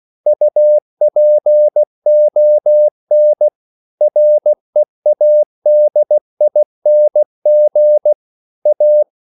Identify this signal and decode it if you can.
Morse code, heard as a keyed tone, about 16 wpm, 600 Hz. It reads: UPON READING A
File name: CW / Practice / EnglishQuotations0416